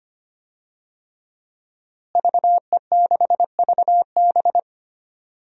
Morse code, heard as a keyed tone, about 25 wpm, 700 Hz. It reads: VE6VB
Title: DXCC-100 Call Signs - 5 Characters - Repeated 3x - 25wpm